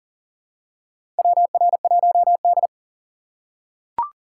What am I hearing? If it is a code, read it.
Morse code, heard as a keyed tone, about 40 wpm, 700 Hz. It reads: WR1D
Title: USA Call Signs - 6x Character Spacing - 40wpm